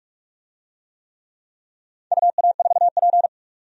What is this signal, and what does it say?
Morse code, heard as a keyed tone, about 45 wpm, 700 Hz. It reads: UA4P
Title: DXCC-100 Call Signs - 4 Characters - 45wpm